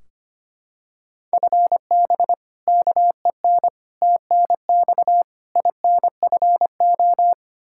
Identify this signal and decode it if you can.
Morse code, heard as a keyed tone, about 25 wpm, 700 Hz. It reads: FB KEN TNX INFO
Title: Practice QSO - Exchange 3 - More Info - 25wpm